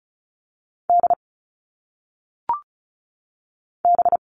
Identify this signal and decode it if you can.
Morse code, heard as a keyed tone, about 34 wpm, 700 Hz. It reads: D B
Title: All Cut Numbers - 1-Digit 35wpm